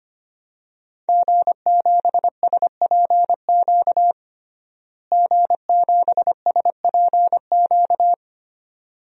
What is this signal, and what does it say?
Morse code, heard as a keyed tone, about 25 wpm, 700 Hz. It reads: G7SPQ G7SPQ
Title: DXCC-100 Call Signs - 5 Characters - Repeated 2x - 25wpm